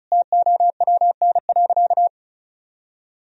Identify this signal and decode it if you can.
Morse code, heard as a keyed tone, about 36 wpm, 700 Hz. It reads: TOWN.
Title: Sentences from Top 1000 Words - Review Part 2 - 35wpm